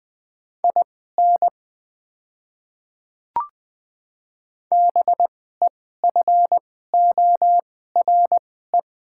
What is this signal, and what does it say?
Morse code, heard as a keyed tone, about 20 wpm, spaced as for 15 wpm, 700 Hz. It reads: IN BEFORE
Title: 2x Character Spacing - Top 200 Words - 20wpm